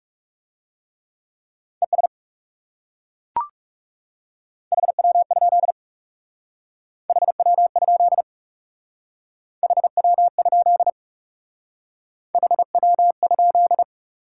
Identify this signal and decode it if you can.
Morse code, heard as a keyed tone, about 37 wpm, 700 Hz. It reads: ES HW? HW? HW? HW?
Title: Speed-Racing Top 90 QSO Elements 30wpm